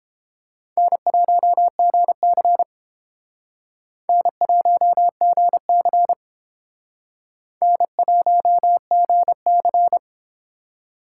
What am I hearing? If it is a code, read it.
Morse code, heard as a keyed tone, about 30 wpm, 700 Hz. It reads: N1GC N1GC N1GC